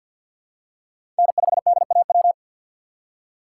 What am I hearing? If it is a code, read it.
Morse code, heard as a keyed tone, about 46 wpm, 700 Hz. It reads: N5DAW
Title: USA Call Signs - 5 Characters - 3x Character Spacing - 50wpm